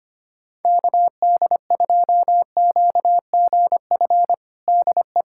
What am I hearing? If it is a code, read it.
Morse code, heard as a keyed tone, about 25 wpm, 700 Hz. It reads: KD2QGF DE
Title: USA Callsign DE Callsign - 6 Characters - 25wpm